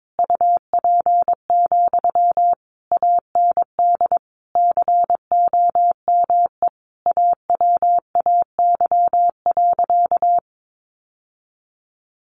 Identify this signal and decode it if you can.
Morse code, heard as a keyed tone, about 22 wpm, 700 Hz. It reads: UP, AND COME AWAY.
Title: Sentences from Top 1000 Words 22wpm